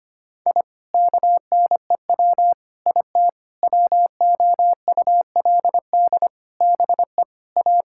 Morse code, heard as a keyed tone, about 25 wpm, 700 Hz. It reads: I KNEW IT WOULD BE A
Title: Sentences from Top 1000 Words - Review Part 6 - 25wpm